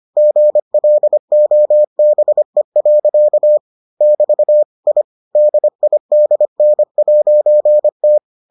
Morse code, heard as a keyed tone, about 25 wpm, 600 Hz. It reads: GLOBE. <BT> I DIDN'T